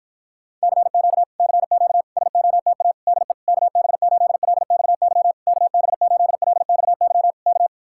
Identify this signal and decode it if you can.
Morse code, heard as a keyed tone, about 52 wpm, 700 Hz. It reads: CQ CQ SOTA DE KB8LXY KB8LXY K